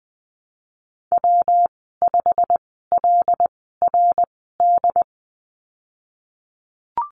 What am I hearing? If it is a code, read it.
Morse code, heard as a keyed tone, about 20 wpm, spaced as for 15 wpm, 700 Hz. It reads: W5LRD
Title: 173 Lesson 28g - Number 9 - Call Signs 20wpm